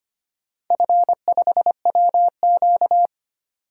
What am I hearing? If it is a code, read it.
Morse code, heard as a keyed tone, about 25 wpm, 700 Hz. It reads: F5WQ
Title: DXCC-100 Call Signs - 4 Characters - 25wpm